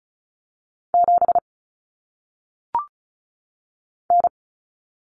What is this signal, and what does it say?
Morse code, heard as a keyed tone, about 36 wpm, 700 Hz. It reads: 7 N
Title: Common Cut Numbers - 1-Digit 35wpm